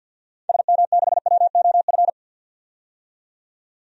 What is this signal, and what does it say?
Morse code, heard as a keyed tone, about 50 wpm, 700 Hz. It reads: SM6WOF
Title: DXCC-100 Call Signs - 6 Characters - 50wpm